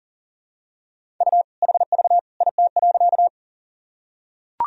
Morse code, heard as a keyed tone, about 40 wpm, 700 Hz. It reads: U HV IT.
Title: Sentences from Top 100 Words and Abbreviations 40wpm